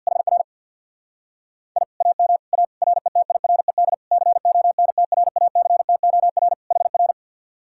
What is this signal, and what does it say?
Morse code, heard as a keyed tone, about 64 wpm, 700 Hz. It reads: HR I AM A RETIRED CONTRACTOR HR